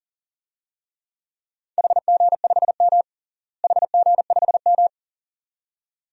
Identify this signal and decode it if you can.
Morse code, heard as a keyed tone, about 37 wpm, 700 Hz. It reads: HG5M HG5M
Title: DXCC-100 Call Signs - 4 Characters - Repeated 2x - 40wpm